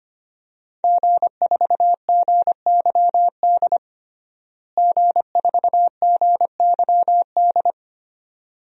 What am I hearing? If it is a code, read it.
Morse code, heard as a keyed tone, about 25 wpm, 700 Hz. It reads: G4GYD G4GYD
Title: DXCC-100 Call Signs - 5 Characters - Repeated 2x - 25wpm